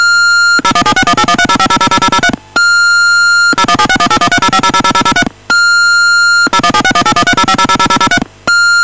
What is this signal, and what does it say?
DTMF, heard as telephone touch tones, tones of about 70 ms. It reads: 0176181305000003 0176181305000003 0176181305000003